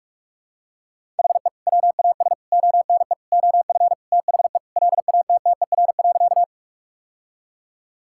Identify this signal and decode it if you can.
Morse code, heard as a keyed tone, about 46 wpm, 700 Hz. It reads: HE WAS ONE OF THE LATTER.
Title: Sentences from Top 1600 Words 45wpm